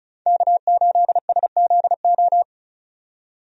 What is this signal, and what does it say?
Morse code, heard as a keyed tone, about 36 wpm, 700 Hz. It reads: K8SZO
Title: USA Call Signs - 5 Characters - 35wpm